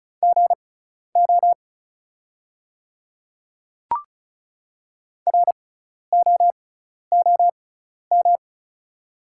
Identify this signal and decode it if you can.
Morse code, heard as a keyed tone, about 36 wpm, spaced as for 12 wpm, 690 Hz. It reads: GO ROOM
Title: 6x Character Spacing - Top 300 Words - 35wpm